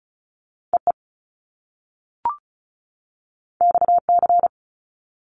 Morse code, heard as a keyed tone, about 35 wpm, 700 Hz. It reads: E E X C
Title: Top Two Letter Combinations in Words 35wpm